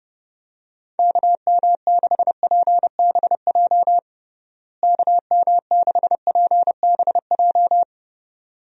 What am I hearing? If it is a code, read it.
Morse code, heard as a keyed tone, about 32 wpm, 700 Hz. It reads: KM6PBJ KM6PBJ
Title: USA Call Signs - 6 Characters - Repeated 2x - 30wpm